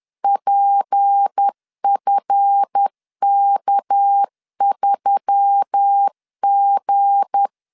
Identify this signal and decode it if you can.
Morse code, heard as a keyed tone, about 10 wpm, 800 Hz. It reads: PFK3G
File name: LEZIONI / 7 GF9Z / 7 GF 10wpm 2514